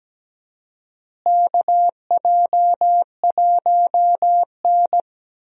Morse code, heard as a keyed tone, about 17 wpm, 700 Hz. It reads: KJ1N
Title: USA Call Signs - 4 Characters - Repeated 3x - 17wpm